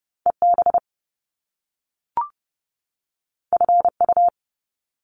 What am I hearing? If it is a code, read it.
Morse code, heard as a keyed tone, about 30 wpm, 700 Hz. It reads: EB FU
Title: Top Two Letter Combinations in Words 30wpm